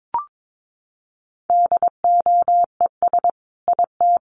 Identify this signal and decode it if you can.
Morse code, heard as a keyed tone, about 22 wpm, 700 Hz. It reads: DOES IT
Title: Sentences from Top 1300 Words 22wpm